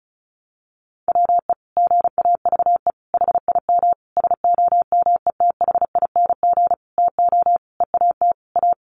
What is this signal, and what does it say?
Morse code, heard as a keyed tone, about 34 wpm, 700 Hz. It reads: WE GAVE HIM SOMETHING TO EAT A
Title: Sentences from Top 600 Words 35wpm